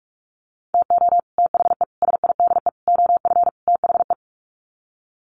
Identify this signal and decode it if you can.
Morse code, heard as a keyed tone, about 44 wpm, 700 Hz. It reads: TO THE SIDE OF THE
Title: Sets of 5 Words 45wpm